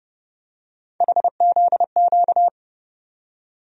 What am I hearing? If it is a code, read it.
Morse code, heard as a keyed tone, about 30 wpm, 700 Hz. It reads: HZQ
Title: USA Call Sign Postfixes - 3 Letters 30wpm